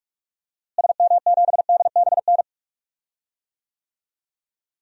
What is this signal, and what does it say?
Morse code, heard as a keyed tone, about 46 wpm, 700 Hz. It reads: SM7DBN
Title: DXCC-100 Call Signs - 6 Characters - Repeated 3x - 45wpm